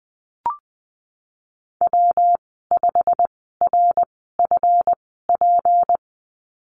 Morse code, heard as a keyed tone, about 20 wpm, spaced as for 15 wpm, 700 Hz. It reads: W5RFP
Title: 161 Lesson 26g - Letter g - Call Signs 20wpm